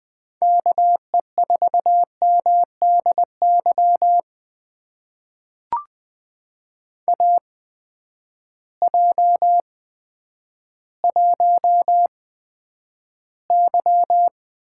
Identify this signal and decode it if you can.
Morse code, heard as a keyed tone, about 20 wpm, spaced as for 10 wpm, 700 Hz. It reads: KE4MDY AJ1Y
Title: USA Call Signs - 8x Character Spacing - 20wpm